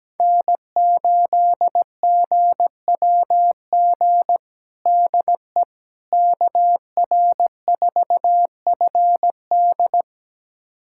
Morse code, heard as a keyed tone, about 17 wpm, 700 Hz. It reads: N8GWG DE KR4FD
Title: USA Callsign DE Callsign - 5 Characters - 17wpm